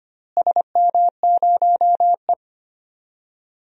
Morse code, heard as a keyed tone, about 25 wpm, 700 Hz. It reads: SM0E
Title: DXCC-100 Call Signs - 4 Characters - Repeated 3x - 25wpm